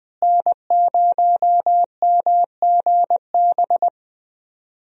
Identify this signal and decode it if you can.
Morse code, heard as a keyed tone, about 20 wpm, 700 Hz. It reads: N0MGB
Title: USA Call Signs - 5 Characters - Repeated 3x - 20wpm